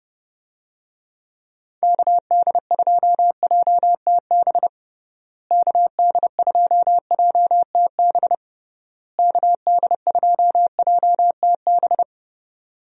Morse code, heard as a keyed tone, about 32 wpm, 700 Hz. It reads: KD2JTB KD2JTB KD2JTB
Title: USA Call Signs - 6 Characters - Repeated 3x - 30wpm